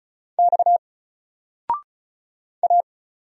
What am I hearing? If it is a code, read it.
Morse code, heard as a keyed tone, about 36 wpm, 700 Hz. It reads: X A
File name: Single-Letter-Number-35wpm